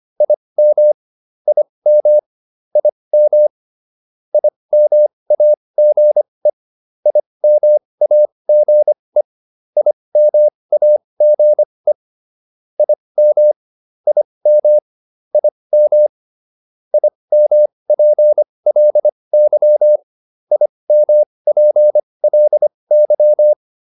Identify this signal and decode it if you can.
Morse code, heard as a keyed tone, about 25 wpm, spaced as for 20 wpm, 600 Hz. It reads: IM IM IM IMAGE IMAGE IMAGE IM IM IM IMPLY IMPLY
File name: IM5 20